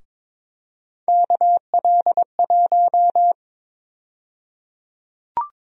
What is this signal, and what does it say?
Morse code, heard as a keyed tone, about 22 wpm, 700 Hz. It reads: KL1
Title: USA Call Sign Prefixes - Repeated 2x - 22wpm